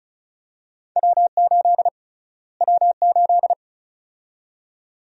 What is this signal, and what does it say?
Morse code, heard as a keyed tone, about 34 wpm, 700 Hz. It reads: W8 W8
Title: USA Call Sign Prefixes - Repeated 2x - 35wpm